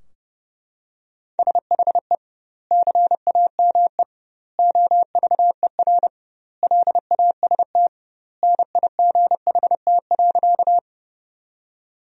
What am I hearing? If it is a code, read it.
Morse code, heard as a keyed tone, about 30 wpm, 700 Hz. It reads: SHE CAME OVER LAST NIGHT.
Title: 2x Word-Spacing - Sentences from Top 300 Words - 30wpm